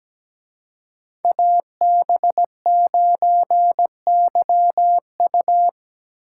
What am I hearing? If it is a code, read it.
Morse code, heard as a keyed tone, about 20 wpm, 700 Hz. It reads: AB9YU
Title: Speed Racing - USA 2x2 Call Signs 17wpm